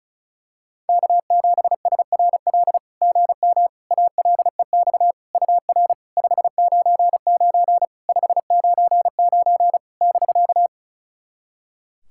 Morse code, heard as a keyed tone, about 36 wpm, 660 Hz. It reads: K7SRL GM ALEX UR 599 599 <BK>
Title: SOTA QSO Practice 35wpm